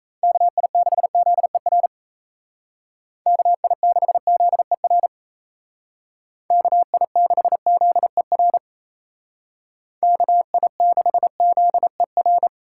Morse code, heard as a keyed tone, about 36 wpm, 700 Hz. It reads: KI6ZER KI6ZER KI6ZER KI6ZER